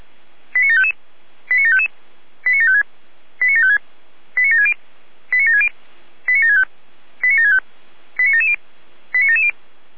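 Five-tone selective call (ZVEI), heard as quick runs of five tones, tones of about 70 ms. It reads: 89755 89755 89756 89756 89760 89760 89765 89765 89770 89770
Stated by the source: ZVEI five-tone selective call received on SDR.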